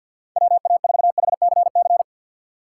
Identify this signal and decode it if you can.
Morse code, heard as a keyed tone, about 52 wpm, 700 Hz. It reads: WA4HCC
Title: K1USN SST QSO Practice - 50wpm